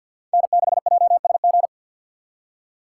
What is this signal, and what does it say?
Morse code, heard as a keyed tone, about 53 wpm, 700 Hz. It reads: N6JSG
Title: USA Call Signs - 5 Characters - 3x Character Spacing - 50wpm